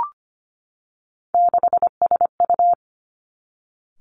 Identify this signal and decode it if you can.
Morse code, heard as a keyed tone, about 25 wpm, 700 Hz. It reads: 6SU
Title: Three Challenging Letter-Number Combinations 25wpm